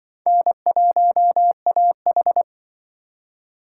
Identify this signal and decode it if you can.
Morse code, heard as a keyed tone, about 24 wpm, 700 Hz. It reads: N1AH
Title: Speed Racing - USA 1x2 Call Signs 20wpm